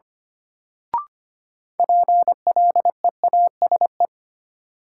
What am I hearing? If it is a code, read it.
Morse code, heard as a keyed tone, about 25 wpm, 700 Hz. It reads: PLEASE
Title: Sentences from Top 1500 Words - New Word Drill - 25wpm